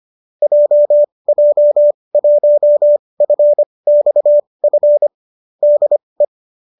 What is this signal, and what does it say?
Morse code, heard as a keyed tone, about 25 wpm, spaced as for 20 wpm, 590 Hz. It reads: JJ1FXF DE
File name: qso208 20